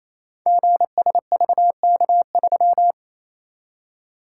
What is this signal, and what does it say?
Morse code, heard as a keyed tone, about 28 wpm, 690 Hz. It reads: GSVK3
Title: Code Groups - 5 Letter:Numbers - 28wpm